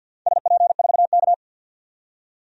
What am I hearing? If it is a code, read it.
Morse code, heard as a keyed tone, about 52 wpm, 700 Hz. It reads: SP4K
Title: DXCC-100 Call Signs - 4 Characters - 50wpm